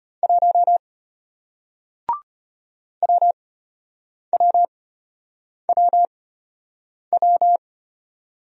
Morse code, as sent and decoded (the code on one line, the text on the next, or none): .---- / .-- .-- .-- .--
1 WWWW